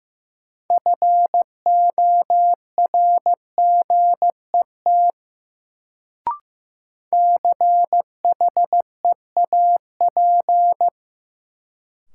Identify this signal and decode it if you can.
Morse code, heard as a keyed tone, about 15 wpm, 700 Hz. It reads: FORGET CHEAP
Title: Sentences from Top 1200 Words - New Word Drill - 15wpm